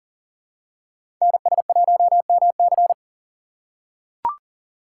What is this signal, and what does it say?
Morse code, heard as a keyed tone, about 40 wpm, 700 Hz. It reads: NS1MC
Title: USA Call Signs - 5 Characters - 4x Character Spacing - 40wpm